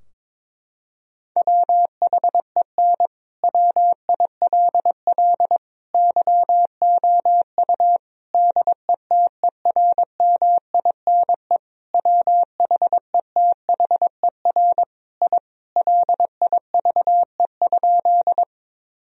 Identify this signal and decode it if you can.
Morse code, heard as a keyed tone, about 22 wpm, 700 Hz. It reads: WHEN WILL YOU DETERMINE WHETHER I LIVE?
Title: Sentences from Top 1000 Words - Review Part 4 - 22wpm